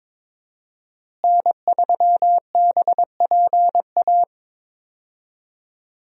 Morse code, heard as a keyed tone, about 22 wpm, 700 Hz. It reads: N3BPA
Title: USA Call Signs - 5 Characters - Repeated 2x - 22wpm